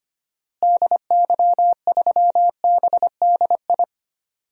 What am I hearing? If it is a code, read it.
Morse code, heard as a keyed tone, about 25 wpm, 700 Hz. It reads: DY3BDI
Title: DXCC-100 Call Signs - 6 Characters - Repeated 3x - 25wpm